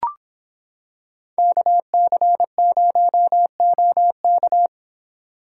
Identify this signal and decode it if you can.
Morse code, heard as a keyed tone, about 26 wpm, 700 Hz. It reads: KC0OK